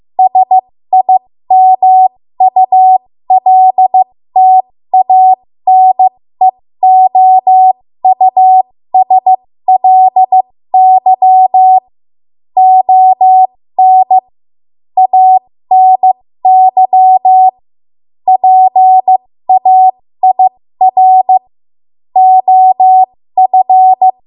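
Morse code, heard as a keyed tone, about 15 wpm, spaced as for 13 wpm, 750 Hz. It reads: SIMULTANEOUSLY ON ANY PAIR OF
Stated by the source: 13 wpm Morse code practice archive transmission.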